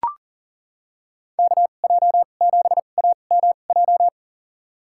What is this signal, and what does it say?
Morse code, heard as a keyed tone, about 40 wpm, spaced as for 29 wpm, 700 Hz. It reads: KJ7AMJ
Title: USA Call Signs - 6 Characters - 2x Character Spacing - 40wpm